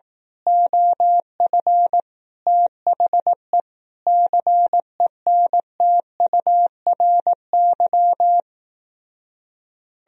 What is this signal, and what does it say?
Morse code, heard as a keyed tone, about 18 wpm, 700 Hz. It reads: OF THE CENTURY
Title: Speed-Racing - Sets of 5 Words 15wpm